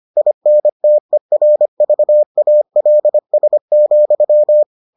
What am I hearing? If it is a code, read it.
Morse code, heard as a keyed tone, about 25 wpm, 600 Hz. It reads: INTERVALS,